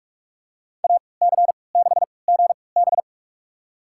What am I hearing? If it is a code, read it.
Morse code, heard as a keyed tone, about 45 wpm, spaced as for 25 wpm, 700 Hz. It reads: AC6GB
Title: USA Call Signs - 5 Characters - 3x Character Spacing - 45wpm